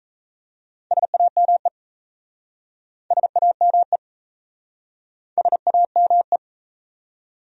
Morse code, heard as a keyed tone, about 37 wpm, 700 Hz. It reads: SAME SAME SAME